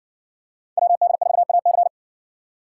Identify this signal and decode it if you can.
Morse code, heard as a keyed tone, about 60 wpm, 700 Hz. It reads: WD4AC